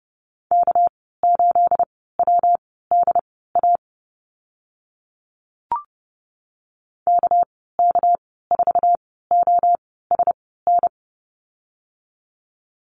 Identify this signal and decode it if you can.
Morse code, heard as a keyed tone, about 30 wpm, spaced as for 17 wpm, 700 Hz. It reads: K8WDA KK4OSN
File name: Lesson 37g - Letter x - Call Signs 30wpm